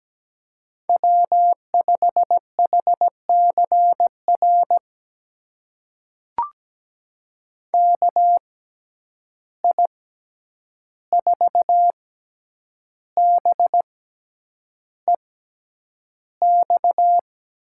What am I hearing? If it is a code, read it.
Morse code, heard as a keyed tone, about 17 wpm, spaced as for 8 wpm, 700 Hz. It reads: W5HCR KI4BEX
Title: USA Call Signs - 6x Character Spacing - 17wpm